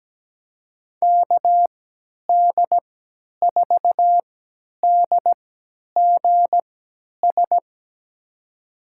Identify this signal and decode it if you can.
Morse code, heard as a keyed tone, about 17 wpm, spaced as for 10 wpm, 700 Hz. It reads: KD4DGS
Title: USA Call Signs - 6 Characters - 3x Character Spacing - 17wpm